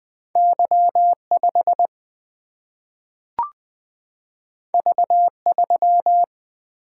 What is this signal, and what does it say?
Morse code, heard as a keyed tone, about 20 wpm, 700 Hz. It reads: Y 5 V 3
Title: Single-Letter Single-Number 20wpm